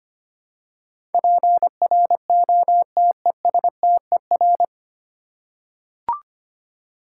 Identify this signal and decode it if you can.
Morse code, heard as a keyed tone, about 25 wpm, 700 Hz. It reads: PROTESTER
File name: practice / 25wpm / 9 Letter Words - First 3 Letters Given - 25wpm